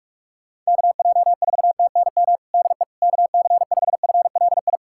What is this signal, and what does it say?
Morse code, heard as a keyed tone, about 46 wpm, 700 Hz. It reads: KJ4TNM DE KC5FLI
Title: USA Callsign DE Callsign - 6 Characters - 45wpm